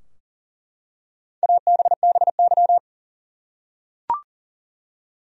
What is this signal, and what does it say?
Morse code, heard as a keyed tone, about 40 wpm, 700 Hz. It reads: ABBY
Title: USA Common Names - 40wpm